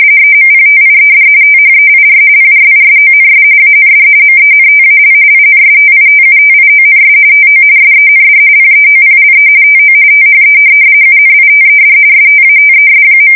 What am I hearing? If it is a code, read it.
RTTY, about 45.45 baud. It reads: EMPTIED THEIR PRISONS AND MENTAL INSTITUTIONS TO SEND PEOPLE TO THE UNITED STATES.
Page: 170 Shift 2125/2295